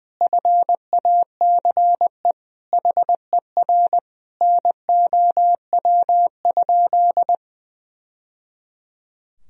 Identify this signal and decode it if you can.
Morse code, heard as a keyed tone, about 20 wpm, 700 Hz. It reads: FACE HER NOW?
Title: Sentences from Top 1000 Words - Review Part 2 - 20wpm